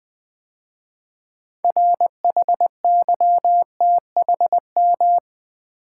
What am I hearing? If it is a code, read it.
Morse code, heard as a keyed tone, about 20 wpm, 700 Hz. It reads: RHYTHM
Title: 6 Letter Words - First 3 Letters Given - 20wpm